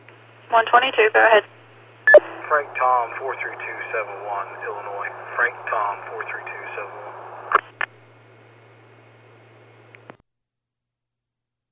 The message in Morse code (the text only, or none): none